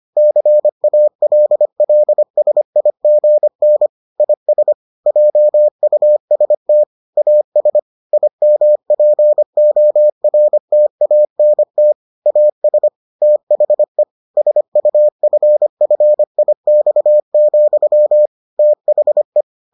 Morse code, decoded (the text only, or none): CALLSIGN IS JUST AS IMPORTANT AS THE SUFFIX, THE